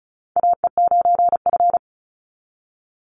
Morse code, heard as a keyed tone, about 36 wpm, 700 Hz. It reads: AE9F
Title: NA SOTA Chaser Callsigns 35wpm